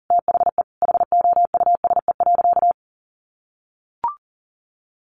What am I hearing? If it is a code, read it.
Morse code, heard as a keyed tone, about 40 wpm, 700 Hz. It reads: THE HOUSE.
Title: Sentences from Top 1400 Words 40wpm